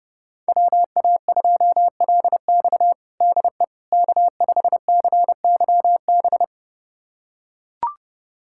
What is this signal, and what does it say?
Morse code, heard as a keyed tone, about 30 wpm, 700 Hz. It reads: WA2LX DE K5CYB
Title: USA Callsign DE Callsign - 5 Characters - 30wpm